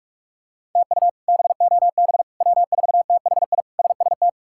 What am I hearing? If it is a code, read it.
Morse code, heard as a keyed tone, about 42 wpm, 700 Hz. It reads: TU BOB W4THI SST
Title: K1USN SST QSO Practice - 45wpm